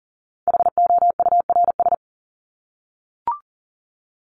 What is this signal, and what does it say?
Morse code, heard as a keyed tone, about 40 wpm, 700 Hz. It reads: HOURS
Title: ES:S Suffix - Top 100 Words with Base Words 40wpm